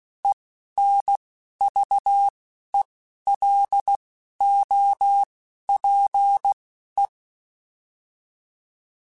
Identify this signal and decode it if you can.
Morse code, heard as a keyed tone, about 16 wpm, spaced as for 11 wpm, 800 Hz. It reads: ENVELOPE
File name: Alphabetical / Alphabetical mp3s / Es @10